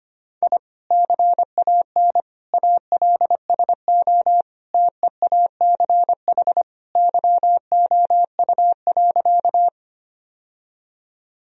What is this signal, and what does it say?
Morse code, heard as a keyed tone, about 25 wpm, 700 Hz. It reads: I CAN ALSO TEACH YOU.
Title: Sentences from Top 1000 Words - Review Part 2 - 25wpm